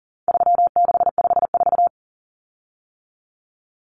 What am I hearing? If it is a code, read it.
Morse code, heard as a keyed tone, about 40 wpm, 700 Hz. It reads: 3654
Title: Common Cut Numbers - 4-Digit 40wpm